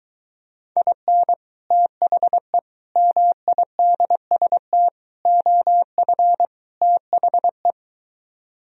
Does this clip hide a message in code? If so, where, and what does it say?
Yes: Morse code from 0.8 to 7.7 s, heard as a keyed tone, about 24 wpm, 700 Hz. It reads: IN THE MIDST OF THE